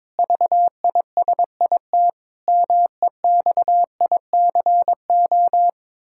Morse code, heard as a keyed tone, about 22 wpm, 700 Hz. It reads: VISIT MEXICO
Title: Pangram Challenge v2 - 22wpm